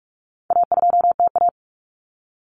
Morse code, heard as a keyed tone, about 44 wpm, 700 Hz. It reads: A2TA
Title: Common Cut Numbers - 4-Digit 45wpm